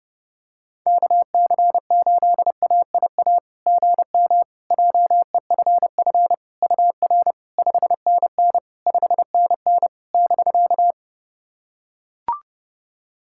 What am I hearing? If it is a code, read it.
Morse code, heard as a keyed tone, about 30 wpm, 700 Hz. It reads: KC8AIA GM JEFF UR 5NN 5NN <BK>
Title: SOTA QSO Practice 30wpm